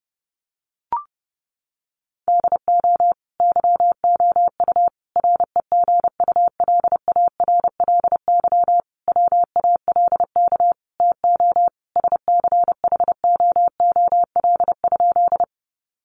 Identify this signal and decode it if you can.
Morse code, heard as a keyed tone, about 30 wpm, 700 Hz. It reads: DO YOU REGULARLY WALK TO SCHOOL?
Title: Sentences from Top 1400 Words 30wpm